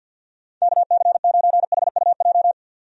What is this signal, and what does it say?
Morse code, heard as a keyed tone, about 50 wpm, 700 Hz. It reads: KC9HUJ
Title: USA Call Signs - 6 Characters - 50wpm